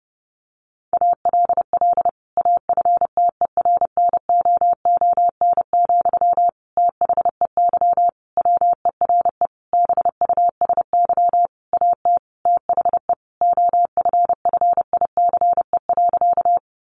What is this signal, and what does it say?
Morse code, heard as a keyed tone, about 30 wpm, 700 Hz. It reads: ALL AFTERNOON, THEY WERE BUSY AT THE OFFICE.
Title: Sentences from Top 1500 Words 30wpm